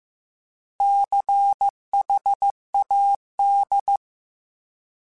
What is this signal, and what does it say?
Morse code, heard as a keyed tone, about 15 wpm, 800 Hz. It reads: CHAD